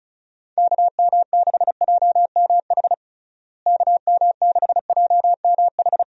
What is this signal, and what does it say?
Morse code, heard as a keyed tone, about 34 wpm, 700 Hz. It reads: KM6JMH KM6JMH
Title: USA Call Signs - 6 Characters - Repeated 3x - 35wpm